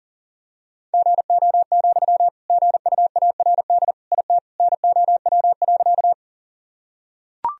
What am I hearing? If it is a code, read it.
Morse code, heard as a keyed tone, about 37 wpm, 700 Hz. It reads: GO, GUARD IT NOW.
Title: Sentences from Top 1000 Words - Review Part 4 - 40wpm